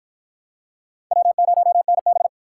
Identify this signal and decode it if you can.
Morse code, heard as a keyed tone, about 53 wpm, 700 Hz. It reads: W0NB